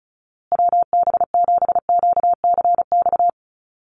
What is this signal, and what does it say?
Morse code, heard as a keyed tone, about 34 wpm, 700 Hz. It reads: WB7QCX
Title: USA Call Signs Repeated 2x 35wpm